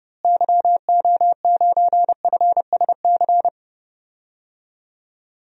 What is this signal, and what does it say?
Morse code, heard as a keyed tone, about 32 wpm, 700 Hz. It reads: YO9FSC
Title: DXCC-100 Call Signs - 6 Characters - Repeated 2x - 30wpm